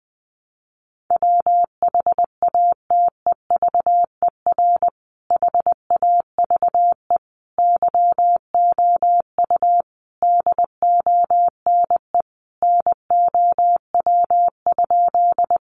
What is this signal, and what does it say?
Morse code, heard as a keyed tone, about 19 wpm, 700 Hz. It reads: WHATEVER HAVE YOU DONE NOW?
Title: Sentences from Top 1100 Words 20wpm